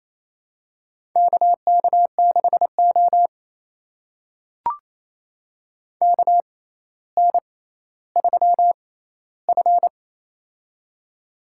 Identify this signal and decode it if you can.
Morse code, heard as a keyed tone, about 28 wpm, spaced as for 14 wpm, 700 Hz. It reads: KK6O KN3F
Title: USA Call Signs - 4 Characters - 6x Character Spacing - 28wpm